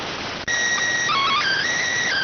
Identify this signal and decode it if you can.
SSTV (slow-scan television), PD 290.